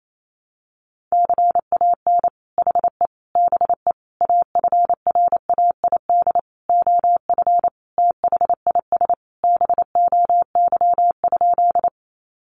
Morse code, heard as a keyed tone, about 28 wpm, 700 Hz. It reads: CAN HE BE AFRAID OF THIS BOY?
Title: Sentences from Top 1500 Words 28wpm